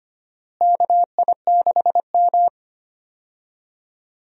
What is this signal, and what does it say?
Morse code, heard as a keyed tone, about 25 wpm, 700 Hz. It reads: KI6M
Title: USA Call Signs - 4 Characters - Repeated 2x - 25wpm